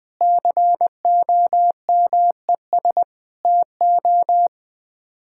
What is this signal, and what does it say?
Morse code, heard as a keyed tone, about 21 wpm, 700 Hz. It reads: COMES TO
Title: Build A Phrase - 5-Word Phrases - 20wpm